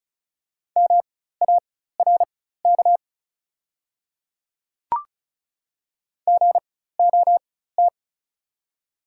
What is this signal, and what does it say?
Morse code, heard as a keyed tone, about 36 wpm, spaced as for 16 wpm, 700 Hz. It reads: MARK GOT
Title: 4x Character Spacing - Top 300 Words - 35wpm